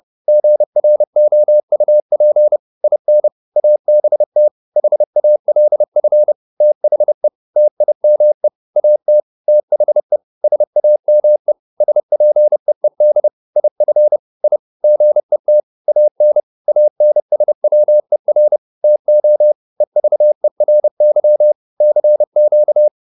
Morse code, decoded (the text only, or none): GROUP IN ABT HALF THE TIME AT THE SAME SPEED IF I GET AN ANSWER TO EVERY CQ